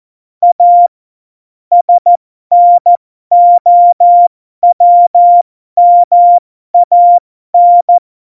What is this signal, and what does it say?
Morse code, heard as a keyed tone, about 14 wpm, spaced as for 12 wpm, 700 Hz. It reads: A SNOWMAN
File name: Morsekurs / Witze